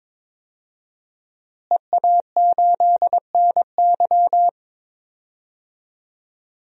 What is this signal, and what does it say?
Morse code, heard as a keyed tone, about 22 wpm, 700 Hz. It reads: EA8NY
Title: DXCC-100 Call Signs - 5 Characters - Repeated 2x - 22wpm